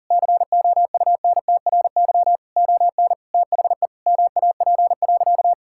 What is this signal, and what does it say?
Morse code, heard as a keyed tone, about 37 wpm, 700 Hz. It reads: COUNTRY ON THE MAP.
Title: Sentences from Top 1000 Words - Review Part 6 - 40wpm